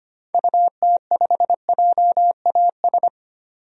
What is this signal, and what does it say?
Morse code, heard as a keyed tone, about 25 wpm, 700 Hz. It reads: UT5JAS
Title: DXCC-100 Call Signs - 6 Characters - 25wpm